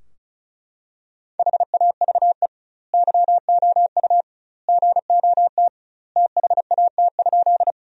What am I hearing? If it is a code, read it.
Morse code, heard as a keyed tone, about 36 wpm, 700 Hz. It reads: HAVE YOU GOT THAT?
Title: 2x Word-Spacing - Sentences from Top 300 Words - 35wpm